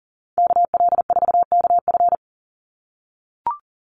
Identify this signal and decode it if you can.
Morse code, heard as a keyed tone, about 40 wpm, 700 Hz. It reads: KL4KF
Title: USA Call Signs 40wpm